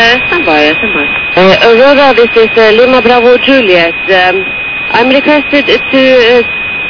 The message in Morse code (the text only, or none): none